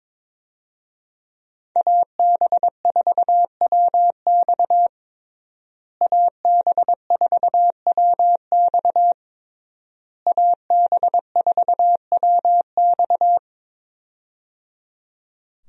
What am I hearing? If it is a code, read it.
Morse code, heard as a keyed tone, about 22 wpm, 700 Hz. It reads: AB4WX AB4WX AB4WX
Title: USA Call Signs - 5 Characters - Repeated 3x - 22wpm